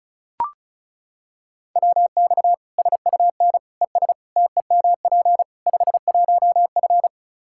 Morse code, heard as a keyed tone, about 36 wpm, 700 Hz. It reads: WX SUN ES TEMP 51F
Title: Practice QSO - Exchange 2 - Info - 35wpm